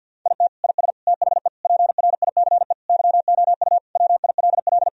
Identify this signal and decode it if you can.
Morse code, heard as a keyed tone, about 50 wpm, 700 Hz. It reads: IT IS THE PRICE YOU WILL
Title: Sentences from Top 1000 Words - Review Part 4 - 50wpm